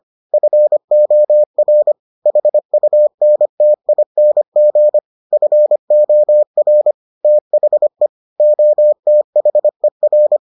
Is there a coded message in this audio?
Yes: Morse code from 0.3 to 10.4 s, heard as a keyed tone, about 25 wpm, 600 Hz. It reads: FOR HUNTING FOR THE OTHER